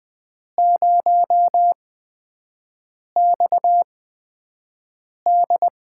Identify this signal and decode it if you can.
Morse code, heard as a keyed tone, about 20 wpm, spaced as for 5 wpm, 700 Hz. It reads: 0XD
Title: USA Call Signs - 5 Characters - 8x Character Spacing - 20wpm